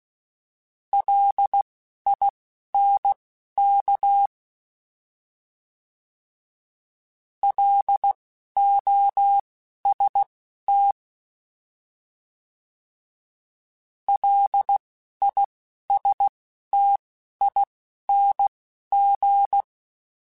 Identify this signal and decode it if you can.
Morse code, heard as a keyed tone, about 16 wpm, spaced as for 11 wpm, 800 Hz. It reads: LINK LOST LISTING
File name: L Runs / L Runs mp3s / L Runs @10